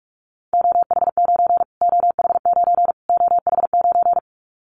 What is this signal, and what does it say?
Morse code, heard as a keyed tone, about 44 wpm, 700 Hz. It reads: OH9 OH9 OH9
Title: Callsign Prefixes for Top 25 DXCC - Repeated 3x - 45wpm